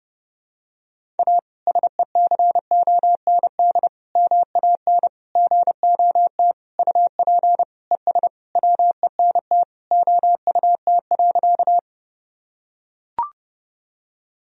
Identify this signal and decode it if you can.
Morse code, heard as a keyed tone, about 30 wpm, 700 Hz. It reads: A SECOND MAN GOT UP ES WENT OUT.
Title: Sentences from Top 300 Words and Abbreviations 30wpm